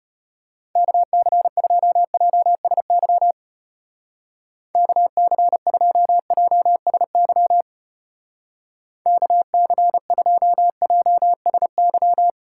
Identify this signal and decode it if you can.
Morse code, heard as a keyed tone, about 29 wpm, 700 Hz. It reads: KC2JSY KC2JSY KC2JSY
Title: Speed Racing - USA Call Signs 25wpm